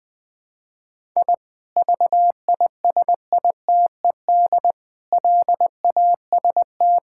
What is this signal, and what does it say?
Morse code, heard as a keyed tone, about 20 wpm, 700 Hz. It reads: I VISITED LAST
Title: Sentences with 2-Word Phrases - 20wpm